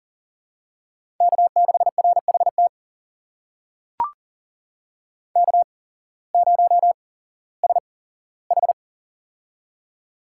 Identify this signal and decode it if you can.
Morse code, heard as a keyed tone, about 40 wpm, spaced as for 19 wpm, 700 Hz. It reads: K6RHT K0SH
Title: USA Call Signs - 8x Character Spacing - 40wpm